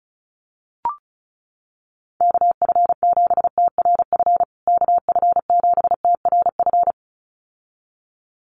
Morse code, heard as a keyed tone, about 34 wpm, 700 Hz. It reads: KF7TRF KF7TRF
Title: USA Call Signs Repeated 2x 35wpm